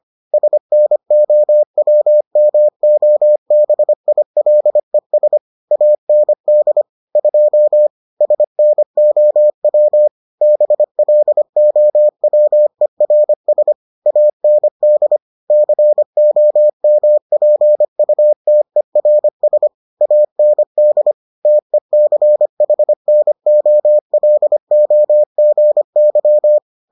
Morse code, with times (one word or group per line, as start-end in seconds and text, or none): 0.3-5.4 SNOWMOBILES
5.7-6.8 AND
7.2-7.9 2
8.2-10.1 SNOW
10.4-13.7 BLOWERS
14.1-15.2 AND
15.5-19.7 COMPUTERS
20.0-21.1 AND
21.5-26.6 TECHNOLOGY